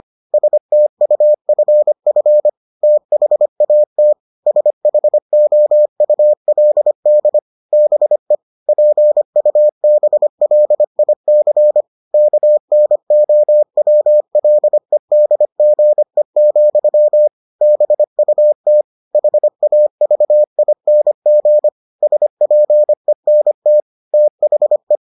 Morse code, heard as a keyed tone, about 25 wpm, 600 Hz. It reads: STUFF THAT SHOULD BE PUBLIC KNOWLEDGE, BUT HAVING SPENT THE